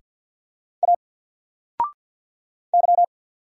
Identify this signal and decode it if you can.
Morse code, heard as a keyed tone, about 48 wpm, 700 Hz. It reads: A Y
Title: Single Letter-Number Alt 50wpm